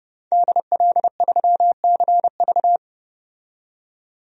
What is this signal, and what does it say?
Morse code, heard as a keyed tone, about 30 wpm, 700 Hz. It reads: DL3CV
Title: DXCC-100 Call Signs - 5 Characters - 30wpm